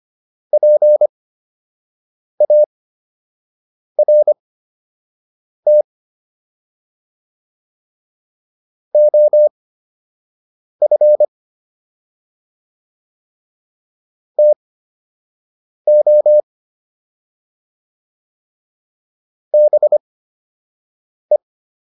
Morse code, heard as a keyed tone, about 25 wpm, spaced as for 6 wpm, 600 Hz. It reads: PART OF TO BE